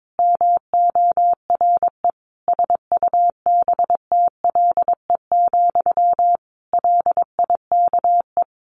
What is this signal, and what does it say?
Morse code, heard as a keyed tone, about 22 wpm, 700 Hz. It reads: MORE SUBTLE, LIKE
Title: A Sound Of Thunder 22wpm